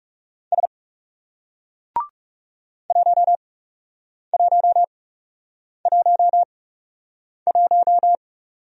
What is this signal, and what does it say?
Morse code, heard as a keyed tone, about 39 wpm, spaced as for 8 wpm, 700 Hz. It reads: S 1111